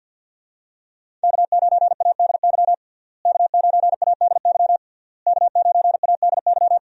Morse code, heard as a keyed tone, about 48 wpm, 700 Hz. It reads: K9ADY K9ADY K9ADY
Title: USA Call Signs - 5 Characters - Repeated 3x - 50wpm